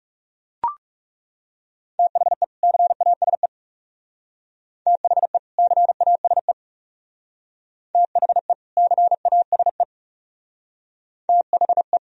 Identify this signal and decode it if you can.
Morse code, heard as a keyed tone, about 34 wpm, 700 Hz. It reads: THE CASE THE CASE THE CASE THE
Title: Speed-Racing - Sets of 2 Words 30wpm